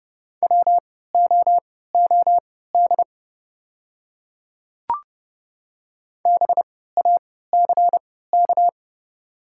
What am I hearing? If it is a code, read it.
Morse code, heard as a keyed tone, about 30 wpm, spaced as for 17 wpm, 700 Hz. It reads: WOOD BACK
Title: 3x Character Spacing - Top 300 Words - 30wpm